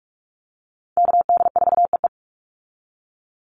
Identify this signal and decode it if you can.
Morse code, heard as a keyed tone, about 44 wpm, 700 Hz. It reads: KD4EE
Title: Call Signs Repeated 3x - 45wpm